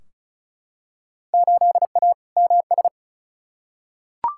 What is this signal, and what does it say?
Morse code, heard as a keyed tone, about 36 wpm, 700 Hz. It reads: 8A MS
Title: Field Day Exchanges - 35wpm